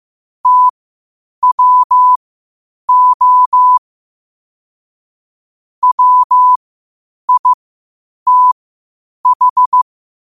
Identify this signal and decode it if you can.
Morse code, heard as a keyed tone, about 15 wpm, spaced as for 8 wpm, 1000 Hz. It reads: TWO WITH
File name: qotdfiles / 8wpm